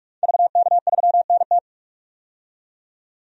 Morse code, heard as a keyed tone, about 46 wpm, 700 Hz. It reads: VK3NT
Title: DXCC-100 Call Signs - 5 Characters - 45wpm